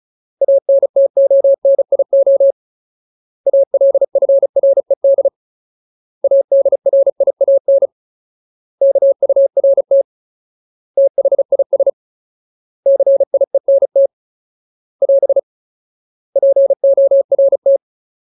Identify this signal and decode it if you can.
Morse code, heard as a keyed tone, about 34 wpm, 550 Hz. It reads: ANTONIO ALFRED ADRIAN KURT THIS CIENT <AS> PORT
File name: Ultimate Word List / 35WPM Ultimate